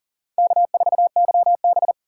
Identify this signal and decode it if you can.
Morse code, heard as a keyed tone, about 40 wpm, 700 Hz. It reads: K4YB
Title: USA Call Signs - 4 Characters - 3x Character Spacing - 40wpm